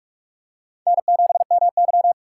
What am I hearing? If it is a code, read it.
Morse code, heard as a keyed tone, about 44 wpm, 700 Hz. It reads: N7MY
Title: USA Call Signs - 4 Characters - 2x Character Spacing - 45wpm